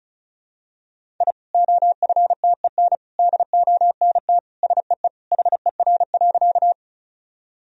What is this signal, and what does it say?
Morse code, heard as a keyed tone, about 36 wpm, 700 Hz. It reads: I OFTEN DONT SEE HER.
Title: Sentences from Top 1000 Words - Review Part 1 - 35wpm